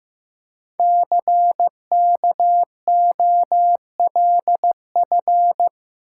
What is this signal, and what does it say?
Morse code, heard as a keyed tone, about 15 wpm, 700 Hz. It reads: CKOLF
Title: Code Groups - 5 Letters - 15wpm